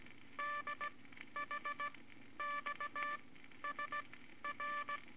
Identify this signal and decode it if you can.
Morse code, heard as a keyed tone, about 18 wpm, spaced as for 12 wpm, 1170 Hz, with noise. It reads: DHXSR
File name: Morse Code Records / 18 B LCM 165 Ten Words Per Minute